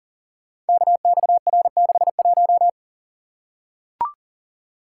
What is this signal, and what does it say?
Morse code, heard as a keyed tone, about 40 wpm, 700 Hz. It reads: KXR61
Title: Code Groups - 5 Letter:Numbers - 40wpm